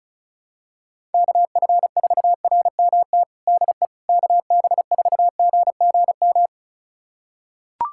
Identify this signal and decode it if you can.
Morse code, heard as a keyed tone, about 36 wpm, 700 Hz. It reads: KF4RMT DE KB4GGM
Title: USA Callsign DE Callsign - 6 Characters - 35wpm